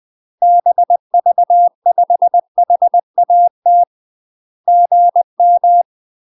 Morse code, heard as a keyed tone, about 20 wpm, 700 Hz. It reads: BV5HAT GM
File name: UK MASTER QSOs TRAINING / 20WPM / UK MASTER QSOs 0072